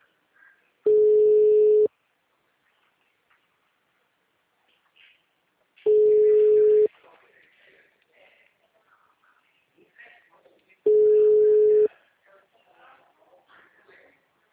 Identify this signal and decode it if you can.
Ringback tone (Europe, 425 Hz), with speech.